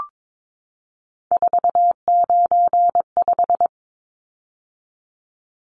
Morse code, heard as a keyed tone, about 22 wpm, 700 Hz. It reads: 495
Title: Numbers - 3-Digit 22wpm